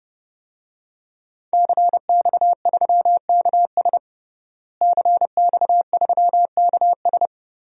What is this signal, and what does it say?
Morse code, heard as a keyed tone, about 31 wpm, 700 Hz. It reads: CX3KS CX3KS
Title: DXCC-100 Call Signs - 5 Characters - Repeated 2x - 30wpm